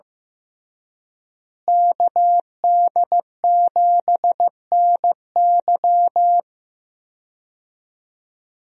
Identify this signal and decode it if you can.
Morse code, heard as a keyed tone, about 15 wpm, 700 Hz. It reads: KD7NY
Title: USA Call Signs - 5 Characters - Repeated 3x - 15wpm